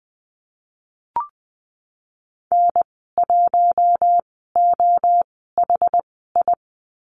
Morse code, heard as a keyed tone, about 20 wpm, spaced as for 14 wpm, 700 Hz. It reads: N1OHI
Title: 088 Lesson 15g - Number 5 - Call Signs 20wpm